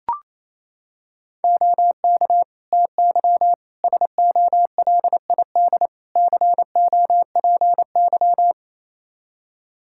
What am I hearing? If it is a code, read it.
Morse code, heard as a keyed tone, about 28 wpm, 700 Hz. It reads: OK TY SOLID COPY
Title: K1USN SST QSO Practice 28wpm